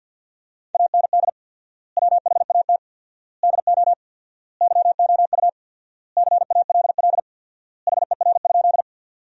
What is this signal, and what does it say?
Morse code, heard as a keyed tone, about 50 wpm, 700 Hz. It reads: AND WHAT DO YOU CALL HER?
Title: 4x Word-Spacing - Sentences from Top 100 Words - 50wpm